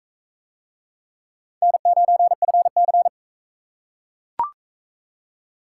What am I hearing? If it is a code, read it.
Morse code, heard as a keyed tone, about 39 wpm, 700 Hz. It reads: N9FC